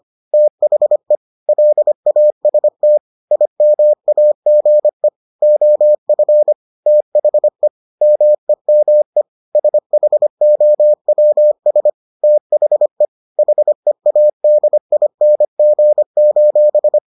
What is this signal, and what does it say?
Morse code, heard as a keyed tone, about 25 wpm, 600 Hz. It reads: THE LAST IMAGE OF THE MEME SHOWS THE HEADING: